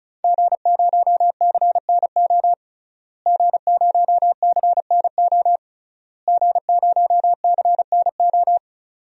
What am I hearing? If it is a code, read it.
Morse code, heard as a keyed tone, about 36 wpm, 700 Hz. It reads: G0CNO G0CNO G0CNO
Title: DXCC-100 Call Signs - 5 Characters - Repeated 3x - 35wpm